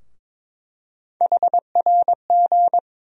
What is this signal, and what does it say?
Morse code, heard as a keyed tone, about 22 wpm, 700 Hz. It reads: HRG
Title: USA Call Sign Postfixes - 3 Letters 22wpm